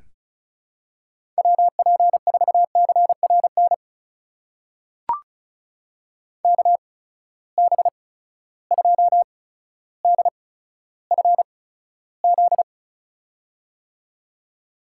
Morse code, heard as a keyed tone, about 36 wpm, spaced as for 15 wpm, 700 Hz. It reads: WP4CRN KB2DFZ
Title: USA Call Signs - 6 Characters - 8x Character Spacing - 35wpm